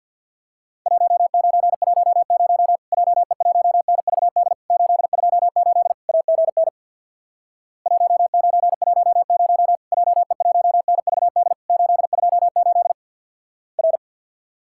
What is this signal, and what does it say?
Morse code, heard as a keyed tone, about 49 wpm, 700 Hz. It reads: 1910 JE1NVD 828 AGN 1910 JE1NVD 828 R
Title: Worked All Europe DX Contest - QTC Exchange with Repeats - 50wpm